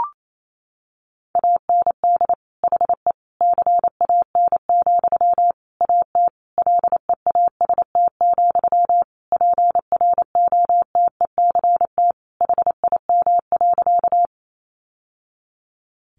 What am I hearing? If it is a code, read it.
Morse code, heard as a keyed tone, about 28 wpm, 700 Hz. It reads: AND HE CAN, AT LEAST, PROTECT HIM.
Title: Sentences from Top 900 Words 28wpm